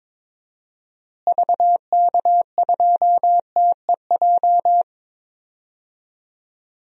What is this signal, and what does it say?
Morse code, heard as a keyed tone, about 22 wpm, 700 Hz. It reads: VK2TEJ
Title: DXCC-100 Call Signs - 6 Characters - Repeated 2x - 22wpm